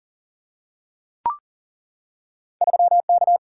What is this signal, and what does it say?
Morse code, heard as a keyed tone, about 40 wpm, 700 Hz. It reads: 3K
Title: Single-Letter Single-Number 40wpm